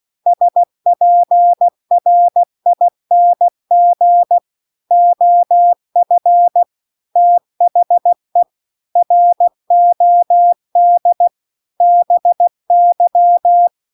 Morse code, heard as a keyed tone, about 16 wpm, 700 Hz. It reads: SPRING OF THE ROD BY
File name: CW-The-Big-Twoheaded-River-2-16WPM